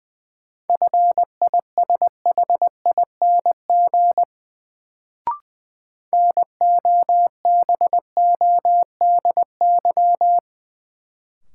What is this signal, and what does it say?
Morse code, heard as a keyed tone, about 20 wpm, 700 Hz. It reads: FISHING NOBODY
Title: Sentences from Top 1600 Words - New Word Drill Alt - 20wpm